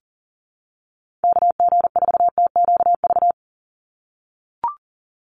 Morse code, heard as a keyed tone, about 40 wpm, 700 Hz. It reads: KG4TQV
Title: USA Call Signs 40wpm